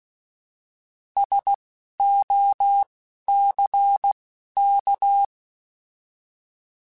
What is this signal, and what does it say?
Morse code, heard as a keyed tone, about 16 wpm, spaced as for 11 wpm, 800 Hz. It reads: SOCK
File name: Alphabetical / Alphabetical mp3s / Ss @10